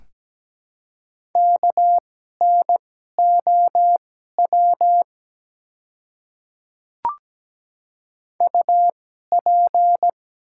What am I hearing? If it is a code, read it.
Morse code, heard as a keyed tone, about 17 wpm, spaced as for 12 wpm, 700 Hz. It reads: KNOW UP
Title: 2x Character Spacing - Top 100 Words - 17wpm